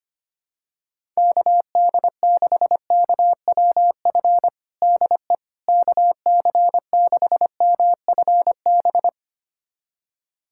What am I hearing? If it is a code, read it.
Morse code, heard as a keyed tone, about 25 wpm, 700 Hz. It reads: KD6KWF DE KC6MFB
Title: USA Callsign DE Callsign - 6 Characters - 25wpm